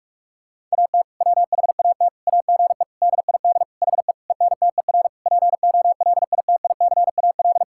Morse code, heard as a keyed tone, about 46 wpm, 700 Hz. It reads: AT WHAT AGE DID HE ENTER POLITICAL
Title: Sentences from Top 1000 Words - Review Part 5 - 45wpm